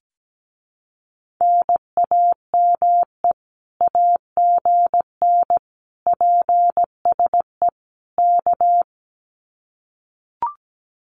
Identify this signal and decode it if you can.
Morse code, heard as a keyed tone, about 17 wpm, 700 Hz. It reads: NAME AGN PSE K
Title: Common QSO Phrases Encore 17wpm